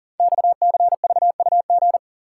Morse code, heard as a keyed tone, about 39 wpm, 700 Hz. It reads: XCVUG
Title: Code Groups - 5 Letter:Numbers - 40wpm